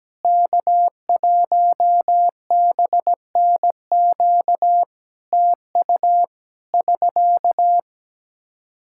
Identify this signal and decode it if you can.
Morse code, heard as a keyed tone, about 17 wpm, 700 Hz. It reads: K1BNQ TU <SK>
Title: K1USN SST QSO Practice 17wpm